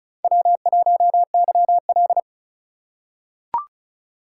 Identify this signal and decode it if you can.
Morse code, heard as a keyed tone, about 36 wpm, 700 Hz. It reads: W1YL
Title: USA Call Signs - 3x Character Spacing - 35wpm